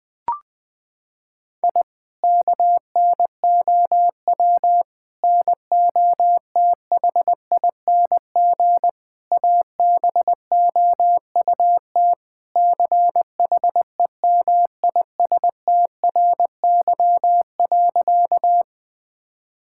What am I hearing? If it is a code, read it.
Morse code, heard as a keyed tone, about 20 wpm, 700 Hz. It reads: I KNOW NOTHING ABOUT CHEMISTRY.
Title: Sentences from Top 1000 Words - Review Part 6 - 20wpm